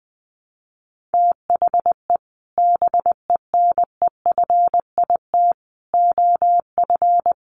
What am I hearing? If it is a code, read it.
Morse code, heard as a keyed tone, about 20 wpm, 700 Hz. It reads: THE BENEFIT OF
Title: Sets of 5 Words Encore 20wpm